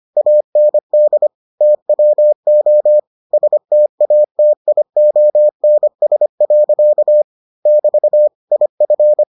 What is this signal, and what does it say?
Morse code, heard as a keyed tone, about 25 wpm, 600 Hz. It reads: AND TWO STATIONS. <BT> IF